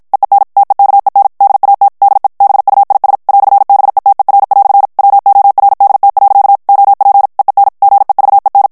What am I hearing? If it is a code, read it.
Morse code, heard as a keyed tone, about 52 wpm, 800 Hz. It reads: EEN TEKEN DAT DE BUIS "BETER" WORDT. OP EEN GEVEN
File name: HOLLANDAIS texte clair 2 vitesse 50 mots minute